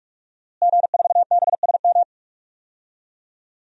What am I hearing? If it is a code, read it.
Morse code, heard as a keyed tone, about 46 wpm, 700 Hz. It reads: G4BSM
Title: DXCC-100 Call Signs - 5 Characters - Repeated 3x - 45wpm